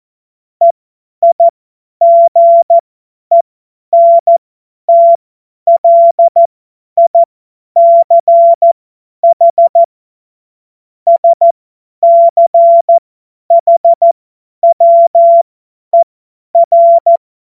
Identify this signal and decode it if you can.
Morse code, heard as a keyed tone, about 14 wpm, spaced as for 10 wpm, 680 Hz. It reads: EIGENTLICH SCHWER
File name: Witz-12-August-2025-50-F70